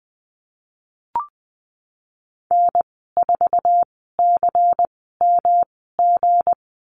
Morse code, heard as a keyed tone, about 19 wpm, spaced as for 14 wpm, 700 Hz. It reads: N4CMG
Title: 179 Lesson 29g - Slash - Call Signs 20wpm